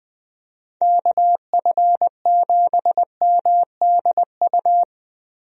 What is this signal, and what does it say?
Morse code, heard as a keyed tone, about 20 wpm, 700 Hz. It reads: KF7MDU
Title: USA Call Signs - 6 Characters - 20wpm